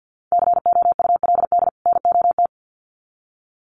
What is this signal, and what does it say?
Morse code, heard as a keyed tone, about 50 wpm, 700 Hz. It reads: COULD NOT
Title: Sets of 2 Words 50wpm